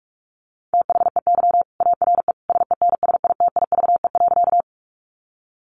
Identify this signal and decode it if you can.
Morse code, heard as a keyed tone, about 44 wpm, 700 Hz. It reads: THEY ARE SENSITIVE.
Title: Sentences from Top 1400 Words 45wpm